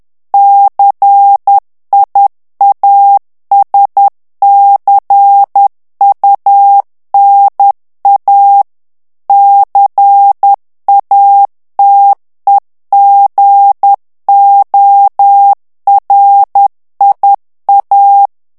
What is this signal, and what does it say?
Morse code, heard as a keyed tone, about 10 wpm, 800 Hz. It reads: CIASCUNA CATEGORIA
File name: ITALIEN texte clair 2 vitesse 10 mots minute